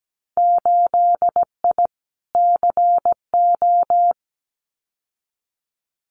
Morse code, heard as a keyed tone, about 17 wpm, 700 Hz. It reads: 8I CO
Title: Winter Field Day Exchanges Encore 17wpm